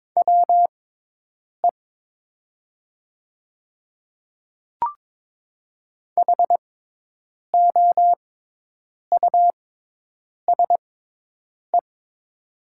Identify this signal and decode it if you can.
Morse code, heard as a keyed tone, about 22 wpm, spaced as for 8 wpm, 700 Hz. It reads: WE HOUSE
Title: 6x Character Spacing - Top 200 Words - 22wpm